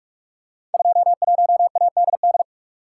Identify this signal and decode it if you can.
Morse code, heard as a keyed tone, about 42 wpm, 700 Hz. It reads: 21ADD
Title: Code Groups - 5 Letter:Numbers - 45wpm